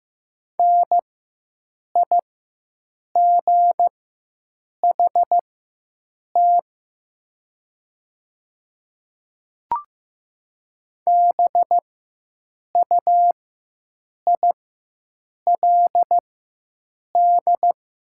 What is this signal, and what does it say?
Morse code, heard as a keyed tone, about 15 wpm, spaced as for 7 wpm, 700 Hz. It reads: NIGHT BUILD
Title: 4x Character Spacing - Top 500 Words - 15wpm